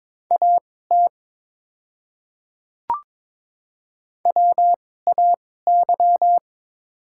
Morse code, heard as a keyed tone, about 22 wpm, spaced as for 16 wpm, 690 Hz. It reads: AT WAY
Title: 2x Character Spacing - Top 100 Words - 22wpm